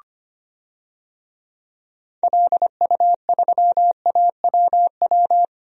Morse code, heard as a keyed tone, about 25 wpm, 700 Hz. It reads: LU3AWW
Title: DXCC-100 Call Signs - 6 Characters - Repeated 2x - 25wpm